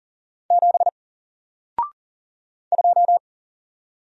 Morse code, heard as a keyed tone, about 39 wpm, 700 Hz. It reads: 7 2
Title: Single Character (Letter:Number:Punctuation:Prosign) Alt 40wpm